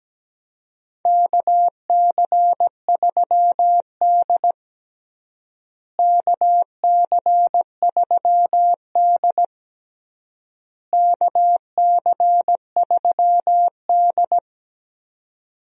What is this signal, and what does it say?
Morse code, heard as a keyed tone, about 17 wpm, 700 Hz. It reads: KC3D KC3D KC3D
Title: USA Call Signs - 4 Characters - Repeated 3x - 17wpm